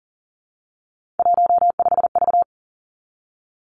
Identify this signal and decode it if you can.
Morse code, heard as a keyed tone, about 41 wpm, 700 Hz. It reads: 15V
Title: Three Challenging Letter-Number Combinations 40wpm